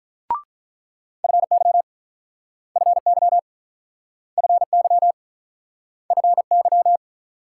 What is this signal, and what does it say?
Morse code, heard as a keyed tone, about 44 wpm, 700 Hz. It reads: FY FY FY FY